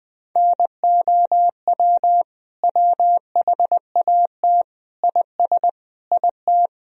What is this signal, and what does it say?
Morse code, heard as a keyed tone, about 20 wpm, 700 Hz. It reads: NOW WHAT IS IT
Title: Sentences from Top 200 Words and Abbreviations 20wpm